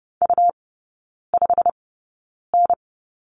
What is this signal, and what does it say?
Morse code, heard as a keyed tone, about 30 wpm, spaced as for 9 wpm, 700 Hz. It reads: U5N
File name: Lesson 19e - Letter w - Three Characters 30wpm